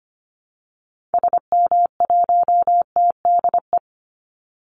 Morse code, heard as a keyed tone, about 25 wpm, 700 Hz. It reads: SM1TDE
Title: Call Signs Repeated 3x - 25wpm